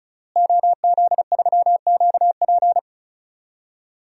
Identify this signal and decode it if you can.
Morse code, heard as a keyed tone, about 36 wpm, 700 Hz. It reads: OZ3QP
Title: DXCC-100 Call Signs - 5 Characters - Repeated 3x - 35wpm